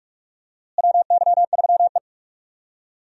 Morse code, heard as a keyed tone, about 42 wpm, 700 Hz. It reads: WY3E
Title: USA Call Signs - 4 Characters - Repeated 3x - 45wpm